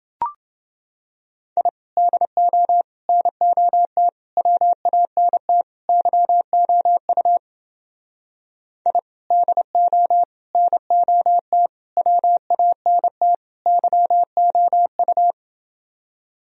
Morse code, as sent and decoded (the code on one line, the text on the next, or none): .. / -.. --- / -. --- - / .-- .- -. - / -.-- --- ..- / .. / -.. --- / -. --- - / .-- .- -. - / -.-- --- ..-
I DO NOT WANT YOU I DO NOT WANT YOU